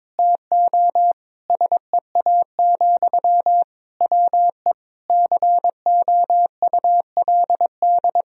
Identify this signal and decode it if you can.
Morse code, heard as a keyed tone, about 22 wpm, 700 Hz. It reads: TO SEA, WE COULD
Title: Sentences from Top 1000 Words - Review Part 2 - 22wpm